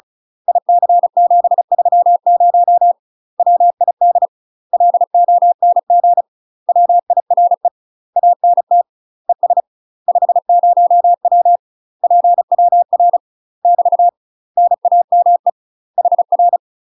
Morse code, heard as a keyed tone, about 35 wpm, 700 Hz. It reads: IC730 WID LONG WIRE ANT ES 50W PWR <BT> NAME HR
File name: UK MASTER QSOs TRAINING / 35WPM / UK MASTER QSOs 0013